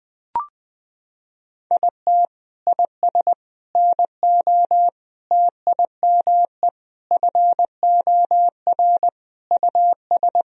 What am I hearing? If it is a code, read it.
Morse code, heard as a keyed tone, about 21 wpm, 700 Hz. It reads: IT IS NO TIME FOR US
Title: Sentences from Top 1000 Words - Review Part 3 - 20wpm